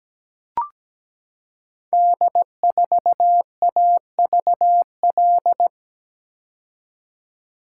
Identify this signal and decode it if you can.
Morse code, heard as a keyed tone, about 17 wpm, 700 Hz. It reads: D4AVL
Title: Code Groups - 5 Letter:Numbers - 17wpm